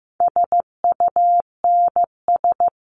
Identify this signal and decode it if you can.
Morse code, heard as a keyed tone, about 15 wpm, 700 Hz. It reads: SUNS
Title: A Sound Of Thunder 15wpm